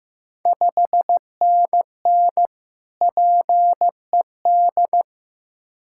Morse code, heard as a keyed tone, about 15 wpm, 690 Hz. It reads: 5NN PED
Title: Ontario QSO Party Exchange 15wpm